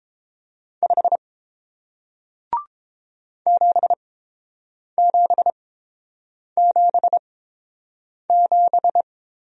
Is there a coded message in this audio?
Yes: Morse code from 0.8 to 9.0 s, heard as a keyed tone, about 30 wpm, spaced as for 7 wpm, 700 Hz. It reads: 5 7777